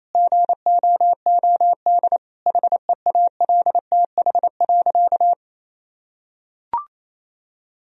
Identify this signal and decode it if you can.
Morse code, heard as a keyed tone, about 28 wpm, 700 Hz. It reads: GOOD HEALTH.
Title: Sentences from Top 1000 Words - Review Part 3 - 28wpm